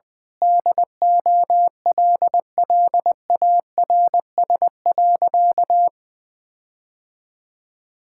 Morse code, heard as a keyed tone, about 20 wpm, 700 Hz. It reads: DOLLARS.
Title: Sentences with 9-Word Phrases - Spoken Beginning - 20wpm